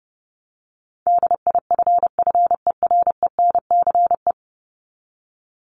Morse code, heard as a keyed tone, about 30 wpm, 700 Hz. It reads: DIFFERENCE
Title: Top 751-1000 Words 30wpm